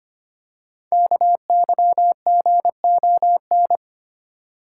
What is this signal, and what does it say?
Morse code, heard as a keyed tone, about 25 wpm, 700 Hz. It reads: KYGON
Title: Code Groups - 5 Letters - 25wpm